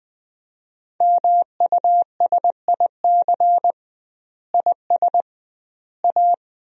Morse code, heard as a keyed tone, about 20 wpm, 700 Hz. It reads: MUSIC IS A
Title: 2x Word-Spacing - Sentences from Top 300 Words - 20wpm